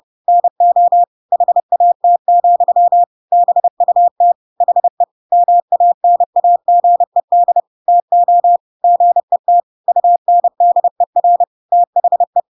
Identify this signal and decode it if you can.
Morse code, heard as a keyed tone, about 30 wpm, 700 Hz. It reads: NO HAT, BUT HE MANAGED TO GET UNDER THE
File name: Pinocchio / Pinocchio ch6 30 30 ik1hge